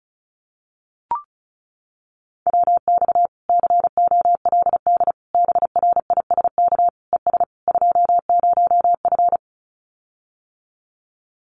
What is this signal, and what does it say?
Morse code, heard as a keyed tone, about 34 wpm, 700 Hz. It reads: WX COLD BRISK ES 20F
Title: Common QSO Phrases Encore 35wpm